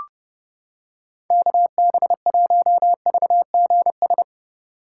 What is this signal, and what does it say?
Morse code, heard as a keyed tone, about 29 wpm, 700 Hz. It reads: KB1VGS